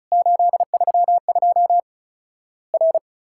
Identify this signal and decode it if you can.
Morse code, heard as a keyed tone, about 36 wpm, 700 Hz. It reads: 832 R
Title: Worked All Europe DX Contest - QTC Exchange - 35wpm